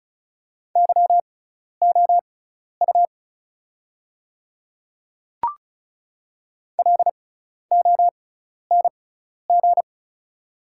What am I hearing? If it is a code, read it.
Morse code, heard as a keyed tone, about 34 wpm, spaced as for 12 wpm, 690 Hz. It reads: YOU LONG
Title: 6x Character Spacing - Top 200 Words - 35wpm